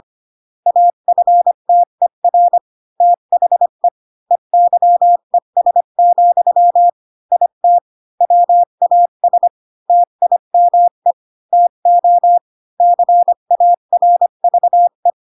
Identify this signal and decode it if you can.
Morse code, heard as a keyed tone, about 25 wpm, 700 Hz. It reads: AFTER THE EYES, IT WAS TIME TO CARVE
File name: Pinocchio / Pinocchio ch3 20 25 ik1hge